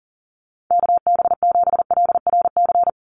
Morse code, heard as a keyed tone, about 40 wpm, 700 Hz. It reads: KB7LRC
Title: USA Call Signs 40wpm